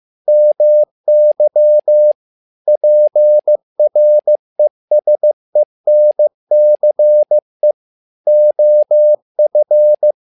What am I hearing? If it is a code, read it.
Morse code, heard as a keyed tone, about 15 wpm, 600 Hz. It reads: MY PRESENCE OF
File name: learning center / tower trouble / tower trouble 15wpm